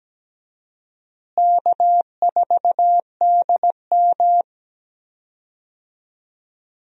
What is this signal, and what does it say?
Morse code, heard as a keyed tone, about 17 wpm, 700 Hz. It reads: K4DM
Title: USA Call Signs - 4 Characters - Repeated 2x - 17wpm